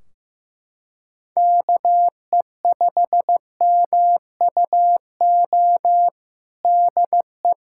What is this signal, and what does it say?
Morse code, heard as a keyed tone, about 15 wpm, 700 Hz. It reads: KE5MUO DE
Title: K1USN SST QSO Practice 15wpm